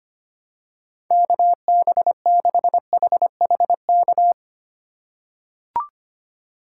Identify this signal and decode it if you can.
Morse code, heard as a keyed tone, about 25 wpm, 700 Hz. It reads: KB6HHK
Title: USA Call Signs - 6 Characters - 3x Character Spacing - 25wpm